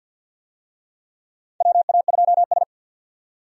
Morse code, heard as a keyed tone, about 50 wpm, 700 Hz. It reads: WA2S
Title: USA Call Signs - 4 Characters - 50wpm